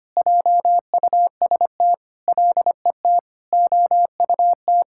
Morse code, heard as a keyed tone, about 25 wpm, 700 Hz. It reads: JUST LET OUT
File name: practice / 25wpm / All-Age Appropriate Jokes - 25wpm